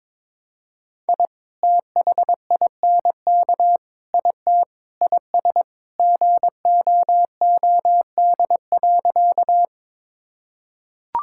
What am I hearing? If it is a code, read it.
Morse code, heard as a keyed tone, about 22 wpm, 700 Hz. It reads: I THINK IT IS GOOD.
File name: practice / 22wpm / Sentences with 3-Word Phrases - 22wpm